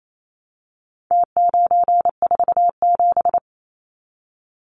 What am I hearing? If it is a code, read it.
Morse code, heard as a keyed tone, about 28 wpm, 700 Hz. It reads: T947
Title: Numbers - Contest Style - 4-Digit 28wpm